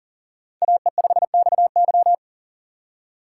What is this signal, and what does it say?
Morse code, heard as a keyed tone, about 40 wpm, 700 Hz. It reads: AE5XY
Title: USA Call Signs - 5 Characters - 4x Character Spacing - 40wpm